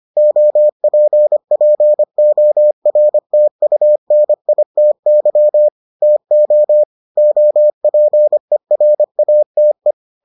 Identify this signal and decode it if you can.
Morse code, heard as a keyed tone, about 25 wpm, 600 Hz. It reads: OPPORTUNITY TO OPERATE